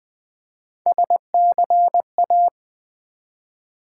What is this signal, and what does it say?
Morse code, heard as a keyed tone, about 20 wpm, 700 Hz. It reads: SCA
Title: Build A Word - 1 Letter At A Time - 5-Letter Words - 20wpm